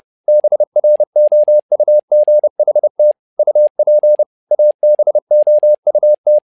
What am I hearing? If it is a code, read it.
Morse code, heard as a keyed tone, about 30 wpm, 600 Hz. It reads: BROUGHT UP ABOUT
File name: learning center / tower trouble / tower trouble 30wpm